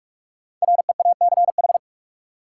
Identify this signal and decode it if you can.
Morse code, heard as a keyed tone, about 45 wpm, 700 Hz. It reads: REACH
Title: Sentences from Top 400 Words - New Word Drill - 45wpm